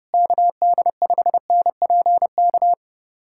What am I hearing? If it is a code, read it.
Morse code, heard as a keyed tone, about 30 wpm, 700 Hz. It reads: KD5NPK
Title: USA Callsign DE Callsign - 6 Characters - 30wpm